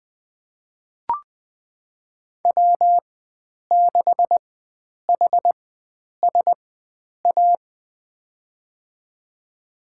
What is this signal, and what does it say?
Morse code, heard as a keyed tone, about 21 wpm, spaced as for 9 wpm, 700 Hz. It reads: W6HSA
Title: USA Call Signs - 5 Characters - 4x Character Spacing - 20wpm